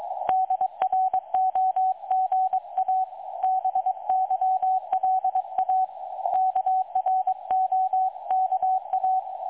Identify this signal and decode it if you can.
Morse code, heard as a keyed tone, about 22 wpm, 750 Hz, with radio noise. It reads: DROGA BYLA KROKA
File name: dlazaawansowanych / nagraniazpasma / websdr recording 3565.0kHz